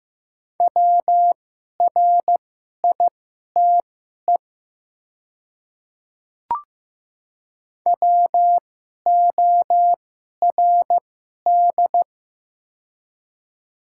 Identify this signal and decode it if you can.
Morse code, heard as a keyed tone, about 15 wpm, spaced as for 11 wpm, 700 Hz. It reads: WRITE WORD
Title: 2x Character Spacing - Top 200 Words - 15wpm